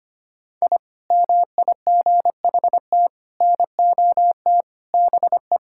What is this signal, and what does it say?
Morse code, heard as a keyed tone, about 25 wpm, 700 Hz. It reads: I MIGHT NOT BE
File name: practice / 25wpm / Sentences with 4-Word Phrases - 25wpm